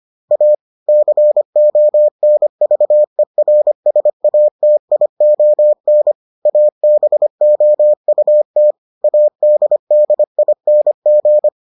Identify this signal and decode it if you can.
Morse code, heard as a keyed tone, about 25 wpm, 600 Hz. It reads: A CONVERSATION ABOUT ADDING